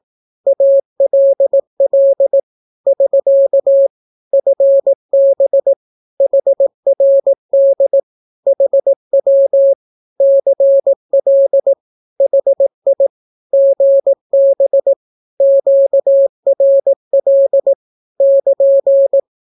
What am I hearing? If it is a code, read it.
Morse code, heard as a keyed tone, about 18 wpm, 550 Hz. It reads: ALL <SK> FB HRD HW CL HI GB QRL <KN>
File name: naukapodstaw / skroty / skroty